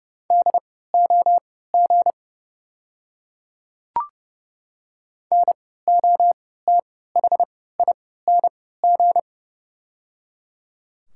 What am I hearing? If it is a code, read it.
Morse code, heard as a keyed tone, about 30 wpm, spaced as for 17 wpm, 690 Hz. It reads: DOG NOTHING
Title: 3x Character Spacing - Top 500 Words - 30wpm